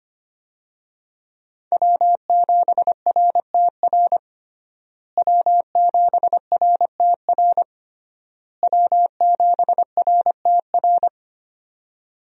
Morse code, heard as a keyed tone, about 25 wpm, 700 Hz. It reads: W7RTR W7RTR W7RTR
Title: USA Call Signs - 5 Characters - Repeated 3x - 25wpm